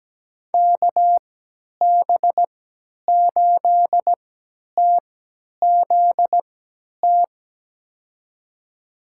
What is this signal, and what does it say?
Morse code, heard as a keyed tone, about 17 wpm, spaced as for 10 wpm, 700 Hz. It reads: KB8TZT
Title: USA Call Signs - 6 Characters - 3x Character Spacing - 17wpm